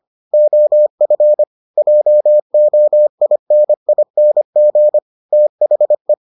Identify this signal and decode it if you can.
Morse code, heard as a keyed tone, about 25 wpm, 600 Hz. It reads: OF JOINING THE